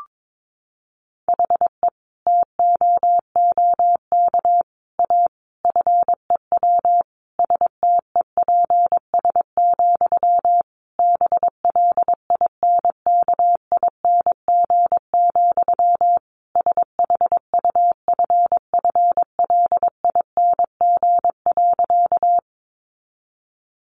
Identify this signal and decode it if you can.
Morse code, heard as a keyed tone, about 22 wpm, 700 Hz. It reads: HE TOOK A FEW STEPS, BLINKING, SHUFFLING.
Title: A Sound Of Thunder 22wpm